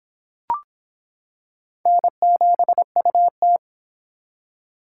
Morse code, heard as a keyed tone, about 25 wpm, 700 Hz. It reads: N7UT